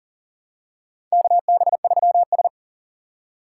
Morse code, heard as a keyed tone, about 40 wpm, 700 Hz. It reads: KB3S
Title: USA Call Signs - 3x Character Spacing - 40wpm